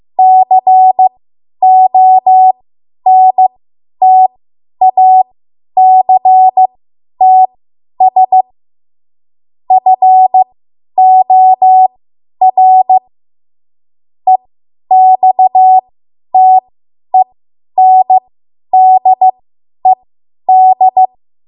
Morse code, heard as a keyed tone, about 15 wpm, spaced as for 10 wpm, 750 Hz. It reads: CONTACTS FOR EXTENDED
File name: Morse / 10WPM